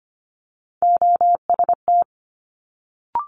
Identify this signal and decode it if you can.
Morse code, heard as a keyed tone, about 25 wpm, 700 Hz. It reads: OST
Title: Multi-letter Phonemes Quick Fire 25wpm